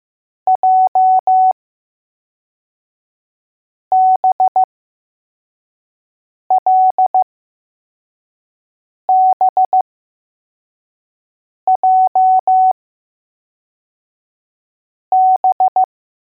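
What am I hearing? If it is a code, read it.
Morse code, heard as a keyed tone, about 15 wpm, spaced as for 4 wpm, 750 Hz. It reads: JBLBJB
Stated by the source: Morse code letters test drill for L, J and B.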